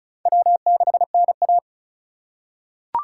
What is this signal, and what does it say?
Morse code, heard as a keyed tone, about 36 wpm, 700 Hz. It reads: W6NA
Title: USA Call Signs - 4 Characters - 3x Character Spacing - 35wpm